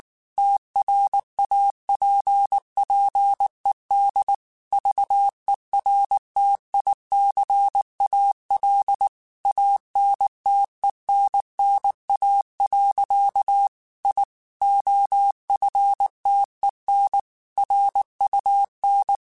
Morse code, heard as a keyed tone, about 19 wpm, 800 Hz. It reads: TRAPPED VERTICAL ANTENNA. I OFTEN RUN
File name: QSOs / QSOs-mp3s / kc7haa@18